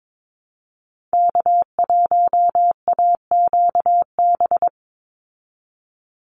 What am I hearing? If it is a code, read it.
Morse code, heard as a keyed tone, about 22 wpm, 700 Hz. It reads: K1AQB
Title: Call Signs Repeated 3x - 22wpm